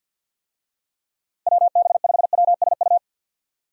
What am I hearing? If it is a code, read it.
Morse code, heard as a keyed tone, about 48 wpm, 700 Hz. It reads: WB5WSU
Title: USA Call Signs - 6 Characters - Repeated 2x - 50wpm